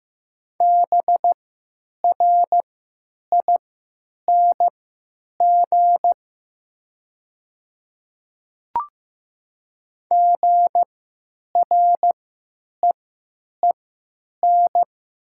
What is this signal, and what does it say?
Morse code, heard as a keyed tone, about 15 wpm, spaced as for 9 wpm, 700 Hz. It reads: BRING GREEN
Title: 3x Character Spacing - Top 500 Words - 15wpm